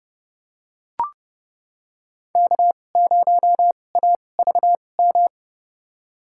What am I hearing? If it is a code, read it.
Morse code, heard as a keyed tone, about 30 wpm, spaced as for 22 wpm, 700 Hz. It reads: K0AVM
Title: USA Call Signs - 5 Characters - 2x Character Spacing - 30wpm